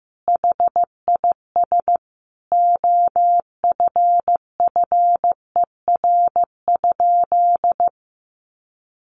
Morse code, heard as a keyed tone, about 15 wpm, 700 Hz. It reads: HIS OFFER?
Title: Sentences from Top 1300 Words 15wpm